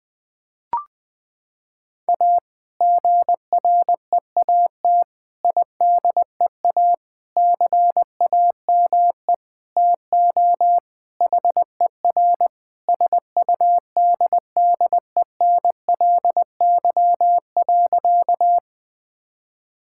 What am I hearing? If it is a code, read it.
Morse code, heard as a keyed tone, about 20 wpm, 700 Hz. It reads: A GREAT IDEA CAME TO HER SUDDENLY.
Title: Sentences with 2-Word Phrases - 20wpm